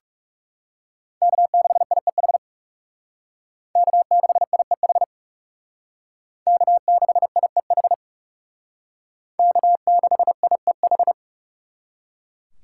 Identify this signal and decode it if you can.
Morse code, heard as a keyed tone, about 36 wpm, 700 Hz. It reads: K6IEH K6IEH K6IEH K6IEH
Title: Speed Racing - USA 1x3 Call Signs 30wpm